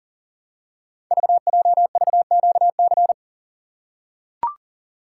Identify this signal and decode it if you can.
Morse code, heard as a keyed tone, about 42 wpm, 690 Hz. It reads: VJVQC
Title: Code Groups - 5 Letters - 40wpm